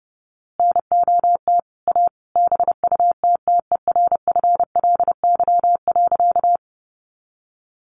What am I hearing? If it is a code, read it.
Morse code, heard as a keyed tone, about 30 wpm, 700 Hz. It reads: NOT A BUTTERFLY.
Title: A Sound Of Thunder 30wpm